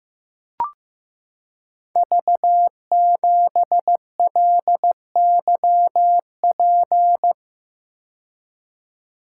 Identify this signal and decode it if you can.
Morse code, heard as a keyed tone, about 15 wpm, 700 Hz. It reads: V7LYP
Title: Code Groups - 5 Letter:Numbers - 15wpm